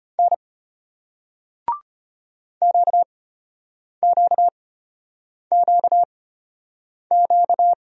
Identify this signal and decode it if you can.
Morse code, heard as a keyed tone, about 33 wpm, spaced as for 8 wpm, 700 Hz. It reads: N QQQQ